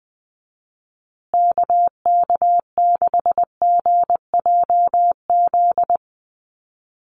Morse code, heard as a keyed tone, about 20 wpm, 700 Hz. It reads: KK6GJZ
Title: USA Call Signs Repeated 3x 20wpm